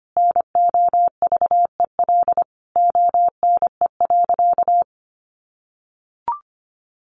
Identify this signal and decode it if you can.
Morse code, heard as a keyed tone, about 25 wpm, 700 Hz. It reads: NOVEL ONE.
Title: Sentences from Top 1300 Words 25wpm